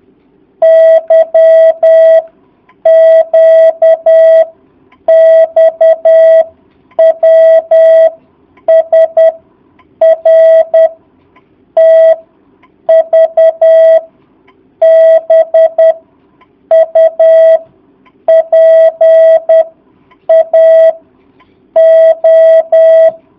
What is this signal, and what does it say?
Morse code, heard as a keyed tone, about 10 wpm, 650 Hz. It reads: Y Q X W S R T V B U P A O
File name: jh8cbh / 2019cw